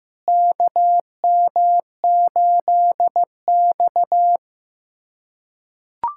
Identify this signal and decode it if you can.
Morse code, heard as a keyed tone, about 15 wpm, 700 Hz. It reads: KM8X
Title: USA Callsign DE Callsign - 4 Characters - 15wpm